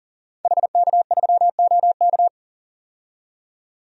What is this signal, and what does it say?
Morse code, heard as a keyed tone, about 40 wpm, 700 Hz. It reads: HK3OK
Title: DXCC-100 Call Signs - 5 Characters - 40wpm